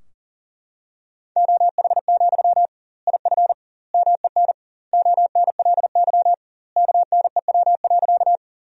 Morse code, heard as a keyed tone, about 40 wpm, 700 Hz. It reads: OH, IF MEN ONLY KNEW.
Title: 2x Word-Spacing - Sentences from Top 500 Words - 40wpm